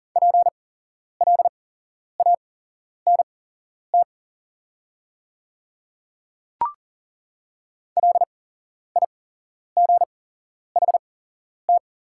Morse code, heard as a keyed tone, about 40 wpm, spaced as for 11 wpm, 700 Hz. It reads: PLANT LIGHT
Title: 8x Character Spacing - Top 300 Words - 40wpm